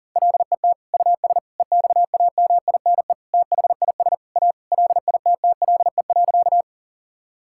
Morse code, heard as a keyed tone, about 40 wpm, 700 Hz. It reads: LET US EXAMINE THIS A LITTLE.
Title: Sentences from Top 1600 Words 40wpm